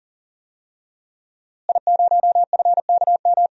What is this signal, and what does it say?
Morse code, heard as a keyed tone, about 40 wpm, 700 Hz. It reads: I0FKM
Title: DXCC-100 Call Signs - 5 Characters - Repeated 3x - 40wpm